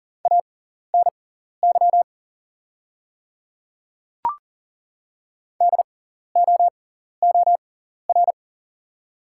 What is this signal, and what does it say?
Morse code, heard as a keyed tone, about 40 wpm, spaced as for 14 wpm, 690 Hz. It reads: ANY DOOR
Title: 6x Character Spacing - Top 300 Words - 40wpm